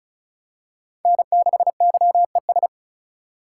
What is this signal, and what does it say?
Morse code, heard as a keyed tone, about 34 wpm, 700 Hz. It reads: N6YES
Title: USA Call Signs - 5 Characters - 2x Character Spacing - 35wpm